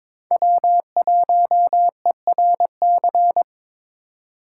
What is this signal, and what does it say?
Morse code, heard as a keyed tone, about 26 wpm, 700 Hz. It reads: W1ERC
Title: Speed Racing - USA 1x3 Call Signs 22wpm